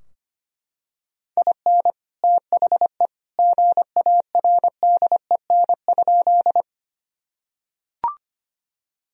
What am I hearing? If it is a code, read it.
Morse code, heard as a keyed tone, about 25 wpm, 700 Hz. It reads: IN THE GARDEN?
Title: Sentences from Top 1000 Words - Review Part 4 - 25wpm